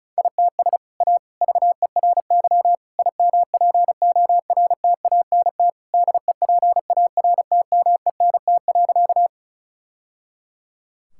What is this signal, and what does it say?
Morse code, heard as a keyed tone, about 34 wpm, 700 Hz. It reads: ITS A VERY IMPORTANT DEPARTMENT.
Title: Sentences from Top 1000 Words - Review Part 5 - 35wpm